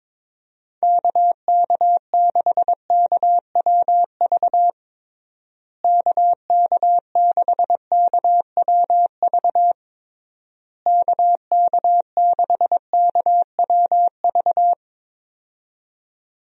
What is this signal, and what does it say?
Morse code, heard as a keyed tone, about 22 wpm, 700 Hz. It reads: KK6KWV KK6KWV KK6KWV
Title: USA Call Signs - 6 Characters - Repeated 3x - 22wpm